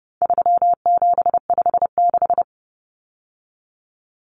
Morse code, heard as a keyed tone, about 30 wpm, 700 Hz. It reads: 3756
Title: Common Cut Numbers - 4-Digit 30wpm